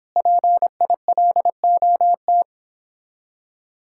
Morse code, heard as a keyed tone, about 29 wpm, 700 Hz. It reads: PILOT